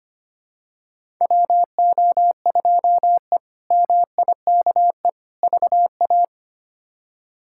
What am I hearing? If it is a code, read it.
Morse code, heard as a keyed tone, about 25 wpm, 700 Hz. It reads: WO2E MIKE VA
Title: K1USN SST QSO Practice - 25wpm